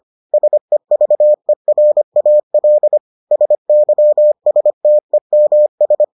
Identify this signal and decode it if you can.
Morse code, heard as a keyed tone, about 25 wpm, 600 Hz. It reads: SEVERAL SYSTEMS